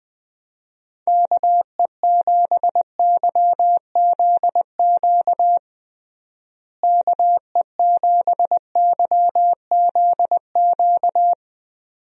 Morse code, heard as a keyed tone, about 20 wpm, 700 Hz. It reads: KE7YZQ KE7YZQ
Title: USA Call Signs - 6 Characters - Repeated 2x - 20wpm